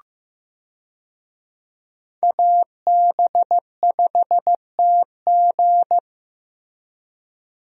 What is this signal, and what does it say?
Morse code, heard as a keyed tone, about 15 wpm, 700 Hz. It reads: AB5TG
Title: USA Call Signs - 5 Characters - Repeated 3x - 15wpm